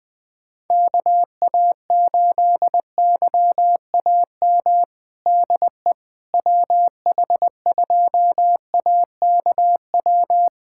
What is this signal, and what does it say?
Morse code, heard as a keyed tone, about 21 wpm, 700 Hz. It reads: KA8YAM DE WH2AKW
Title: USA Callsign DE Callsign - 6 Characters - 20wpm